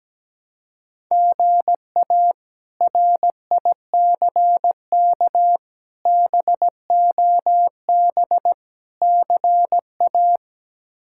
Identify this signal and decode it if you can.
Morse code, heard as a keyed tone, about 17 wpm, 700 Hz. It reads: GA RICK BOB CA
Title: K1USN SST QSO Practice - 17wpm